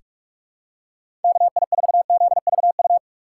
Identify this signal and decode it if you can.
Morse code, heard as a keyed tone, about 46 wpm, 700 Hz. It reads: KI4ZVU
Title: K1USN SST QSO Practice - 45wpm